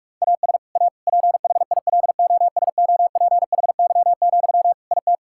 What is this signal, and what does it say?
Morse code, heard as a keyed tone, about 46 wpm, 700 Hz. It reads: AS A PHILOSOPHY, IT
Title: Sentences from Top 1000 Words - Review Part 6 - 45wpm